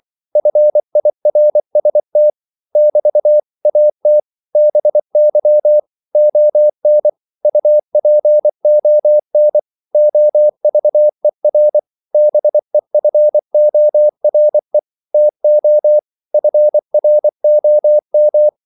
Morse code, heard as a keyed tone, about 24 wpm, 600 Hz. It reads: FIRST <BT> AT BY ON UPON OVER BEFORE TO FROM
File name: CW / Practice / 500MostCommonEnglishWords24